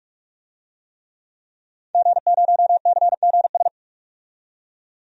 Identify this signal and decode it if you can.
Morse code, heard as a keyed tone, about 46 wpm, 700 Hz. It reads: G0CGS
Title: DXCC-100 Call Signs - 5 Characters - 45wpm